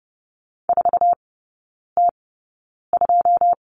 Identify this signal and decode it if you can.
Morse code, heard as a keyed tone, about 32 wpm, spaced as for 9 wpm, 700 Hz. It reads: 4T2
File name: 081 Lesson 14e - Number 2 - Three Characters 30wpm